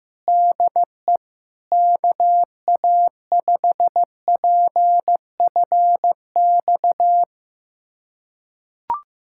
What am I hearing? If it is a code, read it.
Morse code, heard as a keyed tone, about 15 wpm, 700 Hz. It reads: DE KA5PFX
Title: USA Callsign DE Callsign - 6 Characters - 15wpm